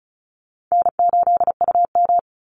Morse code, heard as a keyed tone, about 34 wpm, 700 Hz. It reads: N8UM
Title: Call Signs Repeated 2x - 35wpm